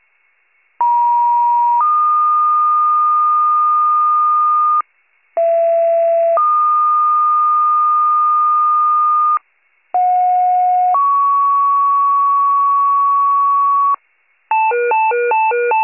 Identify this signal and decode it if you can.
Two-tone page, heard as a one-second tone, then a three-second tone, about 950 then 1220 Hz, about 670 then 1220 Hz, about 730 then 1120 Hz. It is heard from 0.8 to 14.0 s.